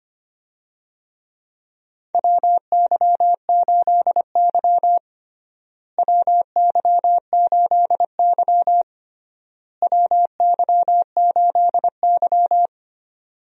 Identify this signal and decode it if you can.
Morse code, heard as a keyed tone, about 25 wpm, 700 Hz. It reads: WY8Y WY8Y WY8Y
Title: USA Call Signs - 4 Characters - Repeated 3x - 25wpm